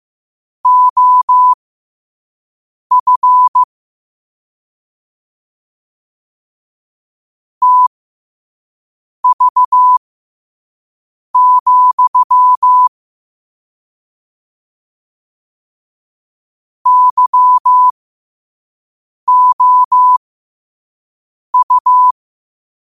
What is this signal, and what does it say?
Morse code, heard as a keyed tone, about 15 wpm, spaced as for 5 wpm, 1000 Hz. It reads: OF TV, YOU
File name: qotdfiles / 5wpm